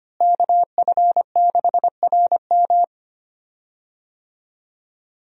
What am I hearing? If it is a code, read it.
Morse code, heard as a keyed tone, about 25 wpm, 700 Hz. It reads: KF6RM
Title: USA Call Signs - 5 Characters - Repeated 3x - 25wpm